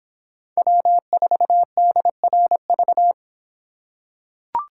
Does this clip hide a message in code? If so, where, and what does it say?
Yes: Morse code from 0.6 to 3.1 s, heard as a keyed tone, about 26 wpm, 700 Hz. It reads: W4DRV